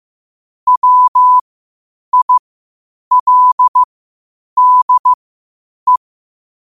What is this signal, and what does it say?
Morse code, heard as a keyed tone, about 15 wpm, spaced as for 8 wpm, 1000 Hz. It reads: WILDE
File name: qotdfiles / 8wpm